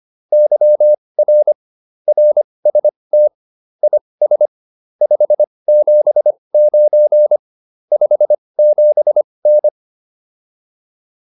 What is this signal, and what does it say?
Morse code, heard as a keyed tone, about 25 wpm, spaced as for 20 wpm, 600 Hz. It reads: YR RST IS 579 57N
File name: QSO304 20